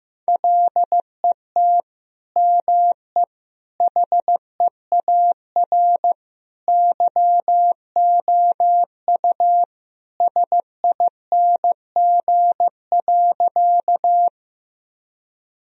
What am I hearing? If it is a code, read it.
Morse code, heard as a keyed tone, about 15 wpm, 700 Hz. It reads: LET ME HEAR YOU SING.
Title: Sentences from Top 1000 Words - Review Part 2 - 15wpm